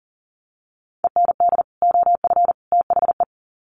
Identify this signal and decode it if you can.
Morse code, heard as a keyed tone, about 40 wpm, 700 Hz. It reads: END OF THE
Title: Sets of 3 Words 40wpm